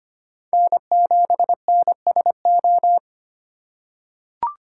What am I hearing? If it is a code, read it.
Morse code, heard as a keyed tone, about 25 wpm, 700 Hz. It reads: N7NSO
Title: USA Call Signs - 5 Characters - 3x Character Spacing - 25wpm